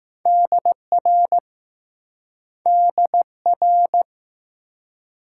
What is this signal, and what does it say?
Morse code, heard as a keyed tone, about 20 wpm, 700 Hz. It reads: DR DR